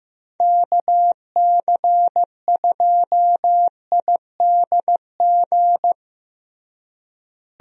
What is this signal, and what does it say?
Morse code, heard as a keyed tone, about 15 wpm, 700 Hz. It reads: KC2IDG
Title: USA Call Signs - 6 Characters - Repeated 2x - 15wpm